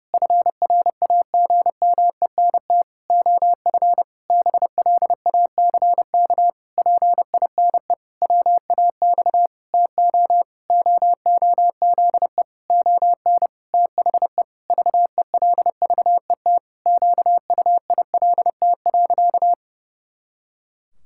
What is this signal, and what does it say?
Morse code, heard as a keyed tone, about 30 wpm, 700 Hz. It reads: FRAGMENT OF BLACK PINE WAX TO OOZE ON THE VELVET QUILT.
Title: Pangram Challenge v2 - 30wpm